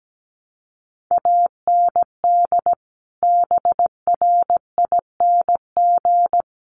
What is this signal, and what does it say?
Morse code, heard as a keyed tone, about 17 wpm, 700 Hz. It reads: AND BRING
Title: Sets of 5 Words Encore 17wpm